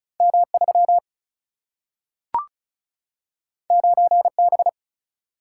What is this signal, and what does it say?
Morse code, heard as a keyed tone, about 36 wpm, 700 Hz. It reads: M 3 9 B
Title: Single-Letter Single-Number 35wpm